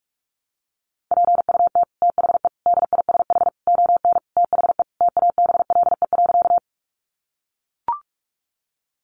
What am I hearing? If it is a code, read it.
Morse code, heard as a keyed tone, about 44 wpm, 700 Hz. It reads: PUT THE DISH ON THE TABLE.
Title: Sentences from Top 1300 Words 45wpm